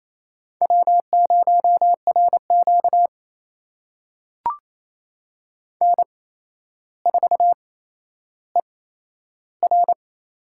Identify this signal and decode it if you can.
Morse code, heard as a keyed tone, about 28 wpm, spaced as for 12 wpm, 700 Hz. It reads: W0RQ N4ER
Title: USA Call Signs - 4 Characters - 8x Character Spacing - 28wpm